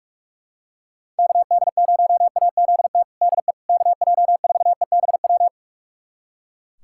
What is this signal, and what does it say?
Morse code, heard as a keyed tone, about 41 wpm, 690 Hz. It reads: KD0AZT DE KJ4EBW
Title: USA Callsign DE Callsign 45wpm